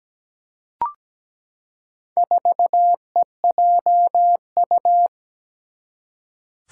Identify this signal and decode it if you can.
Morse code, heard as a keyed tone, about 17 wpm, 700 Hz. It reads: 4EJU
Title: Four Challenging Letter-Number Combinations 17wpm